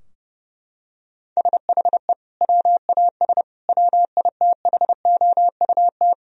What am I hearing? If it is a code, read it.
Morse code, heard as a keyed tone, about 32 wpm, 700 Hz. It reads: SHE WAS WITHOUT
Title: Sentences from Top 1000 Words - Review Part 3 - 30wpm